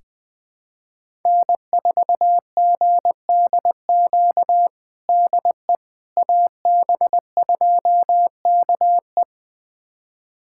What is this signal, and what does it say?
Morse code, heard as a keyed tone, about 20 wpm, 700 Hz. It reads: N4GDQ DE AB2KE
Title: USA Callsign DE Callsign - 5 Characters - 20wpm